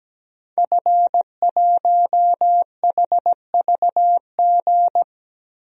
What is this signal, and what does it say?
Morse code, heard as a keyed tone, about 17 wpm, 700 Hz. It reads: F1HVG
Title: DXCC-100 Call Signs - 5 Characters - Repeated 2x - 17wpm